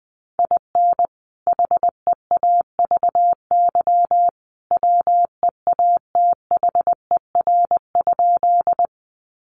Morse code, heard as a keyed tone, about 20 wpm, 700 Hz. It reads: IN HEAVY WEATHER?
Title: Sentences from Top 1100 Words 20wpm